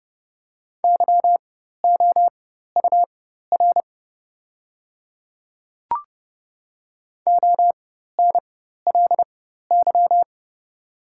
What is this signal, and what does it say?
Morse code, heard as a keyed tone, about 30 wpm, spaced as for 14 wpm, 700 Hz. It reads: YOUR ONLY
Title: 4x Character Spacing - Top 200 Words - 30wpm